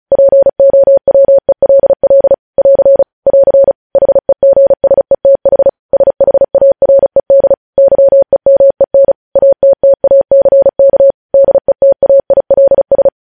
Morse code, heard as a keyed tone, about 36 wpm, 550 Hz. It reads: POWELL <AR> <AR> HEGSETH SHARED YEMEN ATTACK DETAILS
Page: news headlines in morse code at 35 wpm